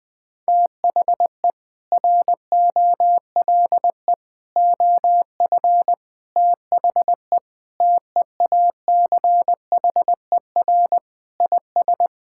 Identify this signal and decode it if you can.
Morse code, heard as a keyed tone, about 20 wpm, 700 Hz. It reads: THE ROLE OF THE TEACHER IS
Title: Sentences with 4-Word Phrases - 20wpm